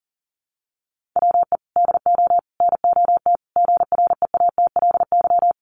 Morse code, heard as a keyed tone, about 40 wpm, 700 Hz. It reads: WE DO NOT GREATLY
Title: Sentences from Top 1400 Words 40wpm